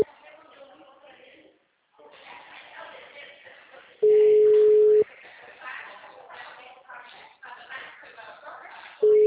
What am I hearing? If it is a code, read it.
Ringback tone (Europe, 425 Hz), with speech.